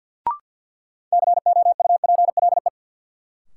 Morse code, heard as a keyed tone, about 49 wpm, 700 Hz. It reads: COUPLE
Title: Sets of 2 Words - New Word Drill - 50wpm